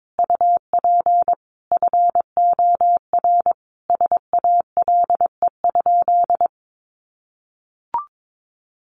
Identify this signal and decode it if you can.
Morse code, heard as a keyed tone, about 22 wpm, 700 Hz. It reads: UP FOR SALE?
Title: Sentences from Top 1400 Words 22wpm